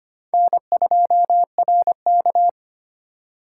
Morse code, heard as a keyed tone, about 25 wpm, 700 Hz. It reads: N2RK
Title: USA Call Signs - 4 Characters - 3x Character Spacing - 25wpm